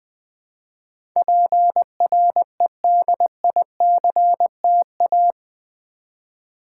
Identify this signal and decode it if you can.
Morse code, heard as a keyed tone, about 20 wpm, 700 Hz. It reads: PREDICTA
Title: Build A Word - 4 Letters At A Time - 11-Letter Words - 20wpm